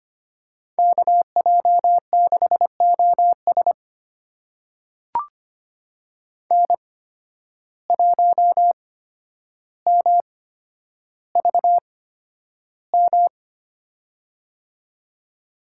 Morse code, heard as a keyed tone, about 25 wpm, spaced as for 11 wpm, 700 Hz. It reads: KJ6OS N1MVM
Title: USA Call Signs - 5 Characters - 8x Character Spacing - 25wpm